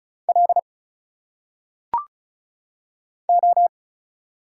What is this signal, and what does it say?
Morse code, heard as a keyed tone, about 36 wpm, 700 Hz. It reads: L O
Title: Single Letters - Phonetic Pronunciation 35wpm